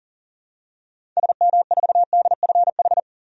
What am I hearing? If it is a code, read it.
Morse code, heard as a keyed tone, about 40 wpm, 700 Hz. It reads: SM4DFH
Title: DXCC-100 Call Signs - 6 Characters - 40wpm